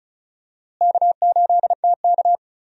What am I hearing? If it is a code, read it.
Morse code, heard as a keyed tone, about 36 wpm, 700 Hz. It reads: K8TK
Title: USA Call Signs - 4 Characters - Repeated 3x - 35wpm